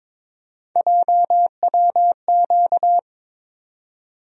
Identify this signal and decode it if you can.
Morse code, heard as a keyed tone, about 22 wpm, 700 Hz. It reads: JWQ
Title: USA Call Sign Postfixes - 3 Letters 22wpm